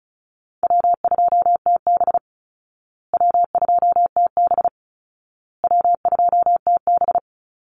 Morse code, heard as a keyed tone, about 34 wpm, 700 Hz. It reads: W2TB W2TB W2TB
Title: Easy Call Signs Repeated 3x - 35wpm